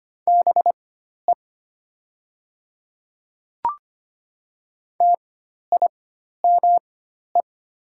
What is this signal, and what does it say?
Morse code, heard as a keyed tone, about 25 wpm, spaced as for 12 wpm, 700 Hz. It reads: BE TIME
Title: 4x Character Spacing - Top 100 Words - 25wpm